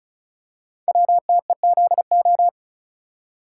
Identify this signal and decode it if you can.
Morse code, heard as a keyed tone, about 36 wpm, 700 Hz. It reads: WTEZO
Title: Code Groups - 5 Letters - 35wpm